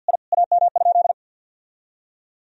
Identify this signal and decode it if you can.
Morse code, heard as a keyed tone, about 46 wpm, 700 Hz. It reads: I AM?
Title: Sentences from Top 1000 Words - Review Part 2 - 50wpm